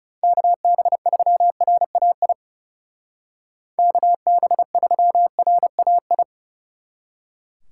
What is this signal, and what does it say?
Morse code, heard as a keyed tone, about 35 wpm, 700 Hz. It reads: KB3RAI KB3RAI
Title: Speed Racing - USA Call Signs 30wpm